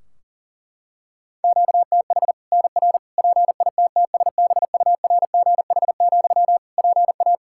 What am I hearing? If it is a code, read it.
Morse code, heard as a keyed tone, about 40 wpm, 700 Hz. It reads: QTH NR PITTSBURGH, PA
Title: Practice QSO - Exchange 1 - Beginning - 40wpm